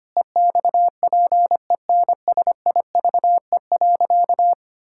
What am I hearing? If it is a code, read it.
Morse code, heard as a keyed tone, about 25 wpm, 700 Hz. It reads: EXPENSIVE.
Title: Pangram Challenge v2 - 25wpm